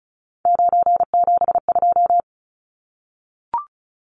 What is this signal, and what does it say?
Morse code, heard as a keyed tone, about 34 wpm, 700 Hz. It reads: 972
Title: Numbers - 3-Digit 35wpm